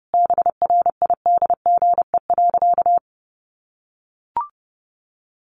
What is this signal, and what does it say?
Morse code, heard as a keyed tone, about 30 wpm, 700 Hz. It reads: BRIDGE.
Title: Sentences from Top 1200 Words 30wpm